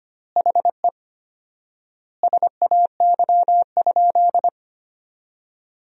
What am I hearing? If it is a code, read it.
Morse code, heard as a keyed tone, about 25 wpm, 700 Hz. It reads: HE SAY?
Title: 4x Word-Spacing - Sentences from Top 300 Words - 25wpm